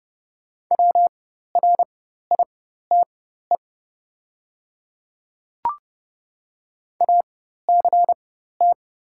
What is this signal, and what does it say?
Morse code, heard as a keyed tone, about 30 wpm, spaced as for 14 wpm, 700 Hz. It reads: WRITE ACT
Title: 4x Character Spacing - Top 200 Words - 30wpm